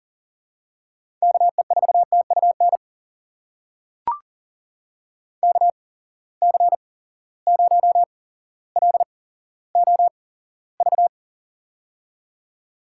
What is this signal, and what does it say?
Morse code, heard as a keyed tone, about 40 wpm, spaced as for 17 wpm, 700 Hz. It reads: KE4TUN KC0LOV
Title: USA Call Signs - 6 Characters - 8x Character Spacing - 40wpm